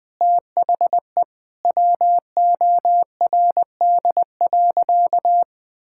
Morse code, heard as a keyed tone, about 21 wpm, 700 Hz. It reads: THE WORD.
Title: Sentences from Top 1000 Words - Review Part 6 - 20wpm